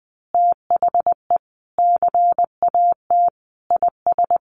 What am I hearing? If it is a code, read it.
Morse code, heard as a keyed tone, about 20 wpm, 700 Hz. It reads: THE CAT IS
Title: Additional Easy Phrases 20wpm